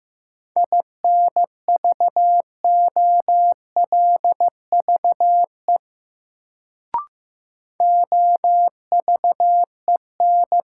Morse code, heard as a keyed tone, about 15 wpm, 700 Hz. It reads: INVOLVE OVEN
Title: Sentences from Top 1200 Words - New Word Drill Alt - 15wpm